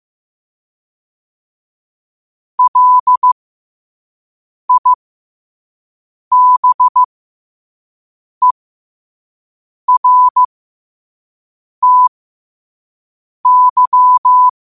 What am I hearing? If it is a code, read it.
Morse code, heard as a keyed tone, about 15 wpm, spaced as for 5 wpm, 1000 Hz. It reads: LIBERTY